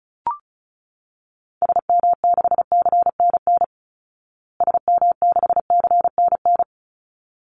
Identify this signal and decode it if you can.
Morse code, heard as a keyed tone, about 34 wpm, 700 Hz. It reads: SM6CNN SM6CNN
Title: Call Signs Repeated 3x - 35wpm